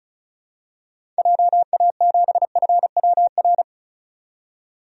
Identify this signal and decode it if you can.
Morse code, heard as a keyed tone, about 34 wpm, 700 Hz. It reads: JA7FWR
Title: DXCC-100 Call Signs - 6 Characters - 35wpm